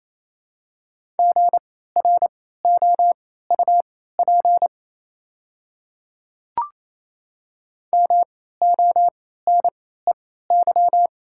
Morse code, heard as a keyed tone, about 28 wpm, spaced as for 16 wpm, 700 Hz. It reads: GROUP MONEY
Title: 3x Character Spacing - Top 500 Words - 28wpm